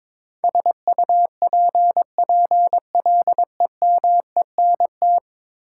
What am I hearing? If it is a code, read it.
Morse code, heard as a keyed tone, about 22 wpm, 700 Hz. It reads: SUPPLEMENT
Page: morse-code-ninja-practice-22wpm